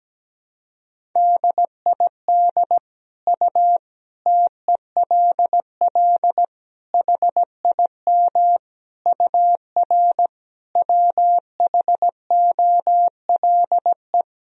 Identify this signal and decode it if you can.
Morse code, heard as a keyed tone, about 17 wpm, 700 Hz. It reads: DID U TELL HIM UR WHOLE
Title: Sentences from Top 400 Words and Abbreviations 17wpm